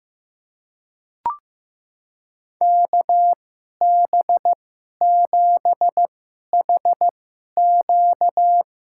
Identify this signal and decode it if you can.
Morse code, heard as a keyed tone, about 15 wpm, spaced as for 11 wpm, 700 Hz. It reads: KB7HQ
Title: USA Call Signs - 2x Character Spacing - 15wpm